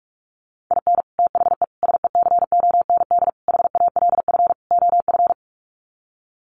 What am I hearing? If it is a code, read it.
Morse code, heard as a keyed tone, about 44 wpm, 700 Hz. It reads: IN THE SECOND HALF OF
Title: Sets of 5 Words 45wpm